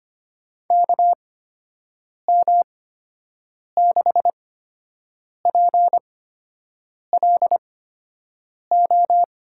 Morse code, heard as a keyed tone, about 25 wpm, spaced as for 7 wpm, 700 Hz. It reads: KM6PLO
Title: USA Call Signs - 8x Character Spacing - 25wpm